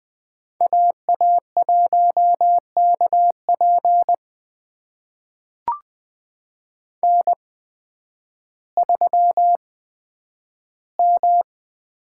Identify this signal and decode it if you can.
Morse code, heard as a keyed tone, about 20 wpm, spaced as for 11 wpm, 700 Hz. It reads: AA1KP N3M
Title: USA Call Signs - 5 Characters - 8x Character Spacing - 20wpm